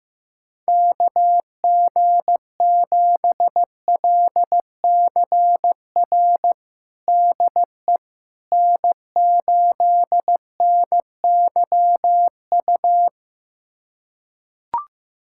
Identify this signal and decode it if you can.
Morse code, heard as a keyed tone, about 15 wpm, 700 Hz. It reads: KG7LCR DE N8NYU
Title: USA Callsign DE Callsign 15wpm